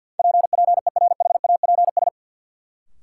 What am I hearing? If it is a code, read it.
Morse code, heard as a keyed tone, about 53 wpm, 700 Hz. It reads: PPERHAPS
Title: Sentences from Top 1100 Words - New Word Drill Alt - 50wpm